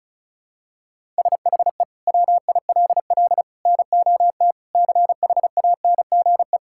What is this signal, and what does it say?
Morse code, heard as a keyed tone, about 36 wpm, 700 Hz. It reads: SHE WILL NOT CHANGE
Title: Sentences from Top 1000 Words - Review Part 6 - 35wpm